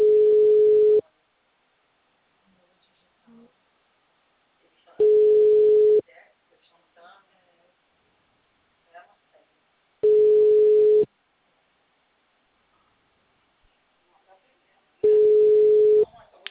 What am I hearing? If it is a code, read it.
Ringback tone (Europe, 425 Hz), with speech.
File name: out- -2001- - - .23227